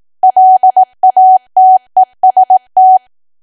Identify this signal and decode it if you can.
Morse code, heard as a keyed tone, about 18 wpm, 750 Hz. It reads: LATEST
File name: cwa / W1AW-Level2 / 18WPM